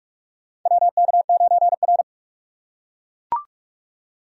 Morse code, heard as a keyed tone, about 44 wpm, 700 Hz. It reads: WK9R
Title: USA Call Signs - 4 Characters - 6x Character Spacing - 45wpm